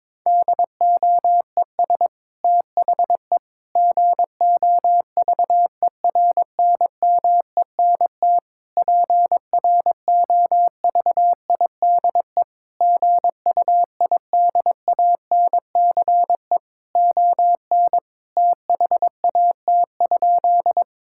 Morse code, heard as a keyed tone, about 22 wpm, 700 Hz. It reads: DOES THE GOVERNMENT PROVIDE GUIDANCE ON THAT?
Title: Sentences from Top 1600 Words 22wpm